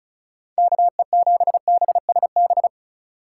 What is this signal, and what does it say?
Morse code, heard as a keyed tone, about 47 wpm, spaced as for 41 wpm, 700 Hz. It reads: KE7BSB
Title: Speed Racing - USA 2x3 Call Signs 35wpm